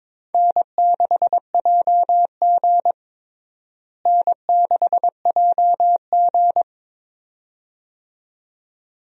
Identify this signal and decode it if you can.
Morse code, heard as a keyed tone, about 22 wpm, 700 Hz. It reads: N6JG N6JG
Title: USA Call Signs - 4 Characters - Repeated 2x - 22wpm